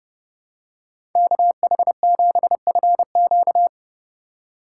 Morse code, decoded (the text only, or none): KH7FQ